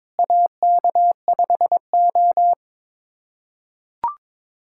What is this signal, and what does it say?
Morse code, heard as a keyed tone, about 22 wpm, 700 Hz. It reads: AK5O
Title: USA Callsign DE Callsign - 4 Characters - 22wpm